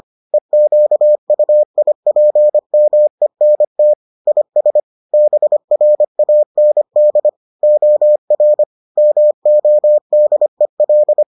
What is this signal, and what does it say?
Morse code, heard as a keyed tone, about 25 wpm, 600 Hz. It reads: EQUIPMENT IS BRAND OR MODEL